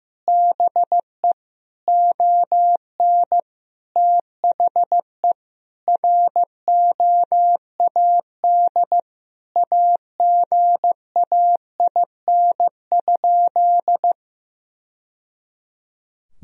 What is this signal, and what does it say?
Morse code, heard as a keyed tone, about 15 wpm, 700 Hz. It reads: BE ON THE ROAD AGAIN?
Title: Sentences from Top 1000 Words - Review Part 2 - 15wpm